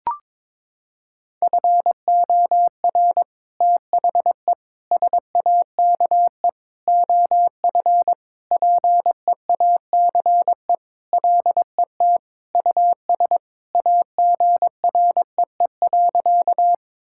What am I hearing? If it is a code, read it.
Morse code, heard as a keyed tone, about 22 wpm, 700 Hz. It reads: FOR THE SAKE OF PEACE LET US AGREE.
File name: practice / 22wpm / Sentences with 4-Word Phrases - 22wpm